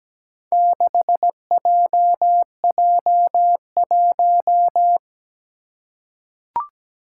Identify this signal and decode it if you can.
Morse code, heard as a keyed tone, about 17 wpm, 700 Hz. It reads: 6JJ1
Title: Four Challenging Letter-Number Combinations 17wpm